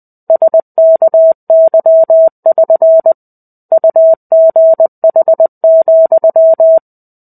Morse code, heard as a keyed tone, about 20 wpm, 650 Hz. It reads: SKY<SN> UGH,
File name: 05-Chapter-05 20 WPM